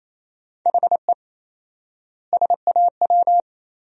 Morse code, heard as a keyed tone, about 27 wpm, 700 Hz. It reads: HE SAW
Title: 4x Word-Spacing - Sentences from Top 500 Words - 28wpm